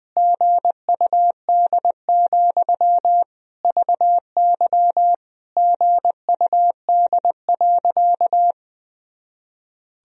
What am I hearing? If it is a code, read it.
Morse code, heard as a keyed tone, about 20 wpm, 700 Hz. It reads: GUD, VY GUD.
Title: Sentences from Top 200 Words and Abbreviations 20wpm